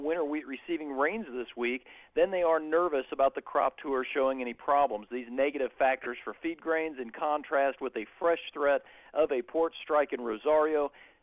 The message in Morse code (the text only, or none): none